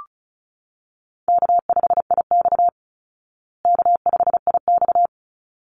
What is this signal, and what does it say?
Morse code, heard as a keyed tone, about 34 wpm, 700 Hz. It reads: K5IX K5IX
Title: Easy Call Signs Repeated 2x - 35wpm